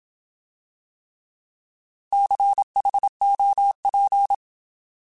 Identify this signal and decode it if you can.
Morse code, heard as a keyed tone, about 27 wpm, 800 Hz. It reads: CHOP
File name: Alphabetical / Alphabetical mp3s / Cs @25